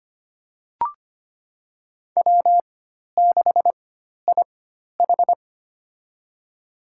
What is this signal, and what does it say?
Morse code, heard as a keyed tone, about 25 wpm, spaced as for 12 wpm, 700 Hz. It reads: W6IH
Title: USA Call Signs - 4 Characters - 4x Character Spacing - 25wpm